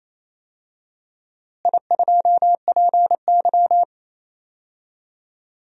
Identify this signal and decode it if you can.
Morse code, heard as a keyed tone, about 28 wpm, 700 Hz. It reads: I2PY
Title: DXCC-100 Call Signs - 4 Characters - Repeated 3x - 28wpm